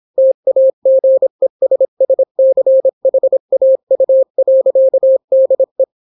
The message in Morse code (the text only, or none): TAGESSCHAU.DE